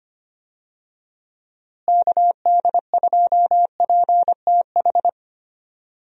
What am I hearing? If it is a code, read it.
Morse code, heard as a keyed tone, about 25 wpm, 700 Hz. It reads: KD2PTH
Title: USA Call Signs - 6 Characters - Repeated 3x - 25wpm